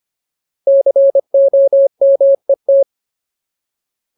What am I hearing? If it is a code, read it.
Morse code, heard as a keyed tone, about 25 wpm, 550 Hz. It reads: COMET
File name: Ultimate Word List / 25WPM Ultimate